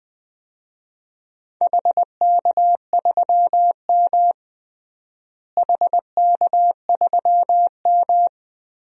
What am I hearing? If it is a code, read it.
Morse code, heard as a keyed tone, about 20 wpm, 700 Hz. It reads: HK3M HK3M
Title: DXCC-100 Call Signs - 4 Characters - Repeated 2x - 20wpm